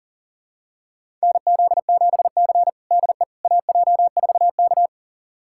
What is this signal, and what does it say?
Morse code, heard as a keyed tone, about 40 wpm, 700 Hz. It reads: NZ7C DE AJ4K
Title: USA Callsign DE Callsign - 4 Characters - 40wpm